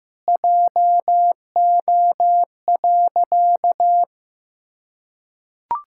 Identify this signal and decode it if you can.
Morse code, heard as a keyed tone, about 15 wpm, 700 Hz. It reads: JO.
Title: Pangram Challenge v2 - 15wpm